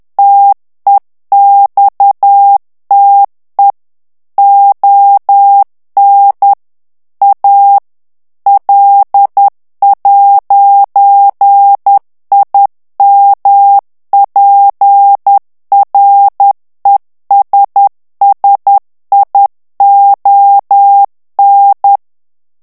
Morse code, heard as a keyed tone, about 10 wpm, 800 Hz. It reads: TEXTE ON A L'IMPRESSION
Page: Français - texte en clair 1 - vitesse : 10 mots minute